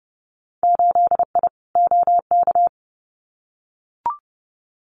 Morse code, heard as a keyed tone, about 30 wpm, 700 Hz. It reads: 8I OK
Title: Winter Field Day Exchanges Encore 30wpm